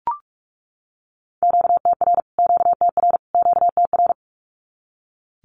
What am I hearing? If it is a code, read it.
Morse code, heard as a keyed tone, about 44 wpm, 700 Hz. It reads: QTR QTR QTR
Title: Q-Codes Repeated 3x 45wpm